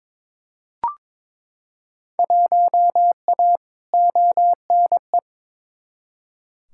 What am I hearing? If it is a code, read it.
Morse code, heard as a keyed tone, about 22 wpm, 700 Hz. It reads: 1A ONE
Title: Field Day Exchanges Encore - 22wpm